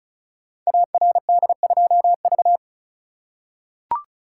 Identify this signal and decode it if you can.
Morse code, heard as a keyed tone, about 36 wpm, 700 Hz. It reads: ARD2V
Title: Code Groups - 5 Letter:Numbers - 35wpm